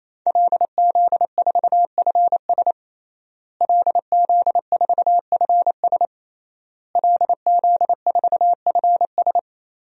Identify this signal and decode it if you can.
Morse code, heard as a keyed tone, about 28 wpm, 700 Hz. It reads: LZ4FS LZ4FS LZ4FS
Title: DXCC-100 Call Signs - 5 Characters - Repeated 3x - 28wpm